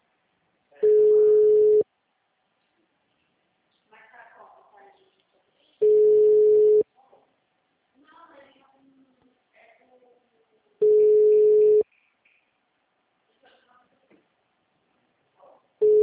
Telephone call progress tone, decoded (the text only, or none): ringback tone